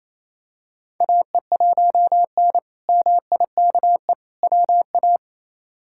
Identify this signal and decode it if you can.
Morse code, heard as a keyed tone, about 28 wpm, 700 Hz. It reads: AE1N MIKE WA
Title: K1USN SST QSO Practice - 28wpm